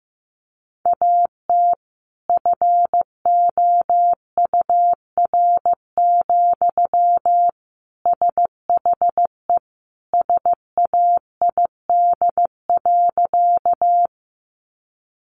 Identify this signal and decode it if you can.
Morse code, heard as a keyed tone, about 15 wpm, 700 Hz. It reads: AT FOUR, SHE SAID.
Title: Sentences from Top 300 Words 15wpm